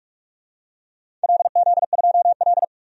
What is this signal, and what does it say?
Morse code, heard as a keyed tone, about 44 wpm, 700 Hz. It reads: LZ2L
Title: DXCC-100 Call Signs - 4 Characters - 45wpm